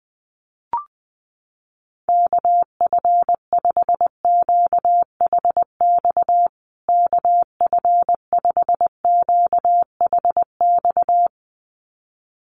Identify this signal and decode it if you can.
Morse code, heard as a keyed tone, about 20 wpm, 700 Hz. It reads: KF5QHX KF5QHX
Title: USA Call Signs Repeated 2x 20wpm